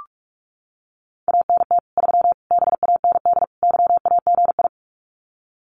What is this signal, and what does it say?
Morse code, heard as a keyed tone, about 44 wpm, 700 Hz. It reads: ANT 3 BAND YAGI
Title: Common QSO Phrases 45wpm